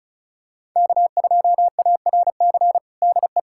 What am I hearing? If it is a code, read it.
Morse code, heard as a keyed tone, about 34 wpm, 690 Hz. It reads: K2ARC DE
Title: USA Callsign DE Callsign - 5 Characters - 35wpm